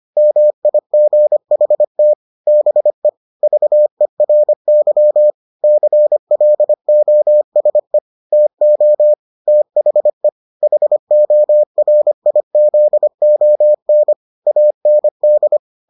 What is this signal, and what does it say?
Morse code, heard as a keyed tone, about 25 wpm, 600 Hz. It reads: MIGHT BE VERY CLOSE TO THE HORIZON AND